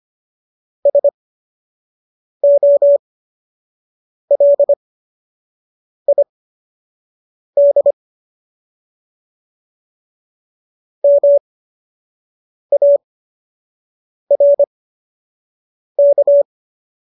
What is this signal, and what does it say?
Morse code, heard as a keyed tone, about 25 wpm, spaced as for 6 wpm, 570 Hz. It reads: SOLID MARK